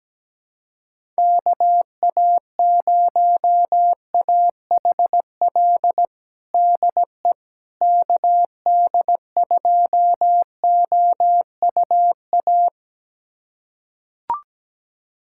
Morse code, heard as a keyed tone, about 17 wpm, 700 Hz. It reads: KA0AHL DE KD2OUA
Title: USA Callsign DE Callsign 17wpm